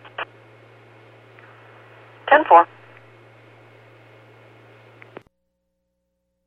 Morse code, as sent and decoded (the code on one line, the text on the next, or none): none